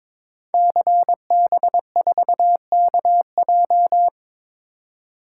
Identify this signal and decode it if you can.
Morse code, heard as a keyed tone, about 22 wpm, 700 Hz. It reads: CB4KJ
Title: Code Groups - 5 Letter:Numbers - 22wpm